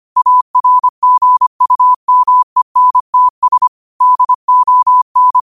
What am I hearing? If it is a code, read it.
Morse code, heard as a keyed tone, about 25 wpm, 1000 Hz. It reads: ARGUMENTS DON